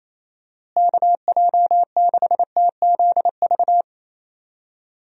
Morse code, heard as a keyed tone, about 28 wpm, 700 Hz. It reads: KJ6TZV
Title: USA Call Signs - 6 Characters - Repeated 2x - 28wpm